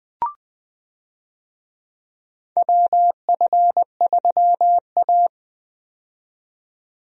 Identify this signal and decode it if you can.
Morse code, heard as a keyed tone, about 20 wpm, 700 Hz. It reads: WF3A
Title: USA Call Signs - 4 Characters - 20wpm